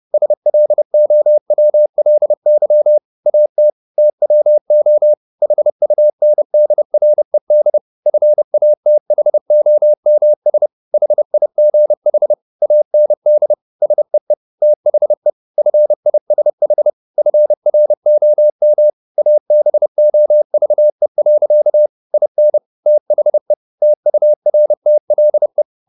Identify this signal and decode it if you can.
Morse code, heard as a keyed tone, about 30 wpm, 600 Hz. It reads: SLOWLY AT TWO HUNDRED FATHOMS HIGH AND SEE THE FISH FROM ABOVE. IN THE TURTLE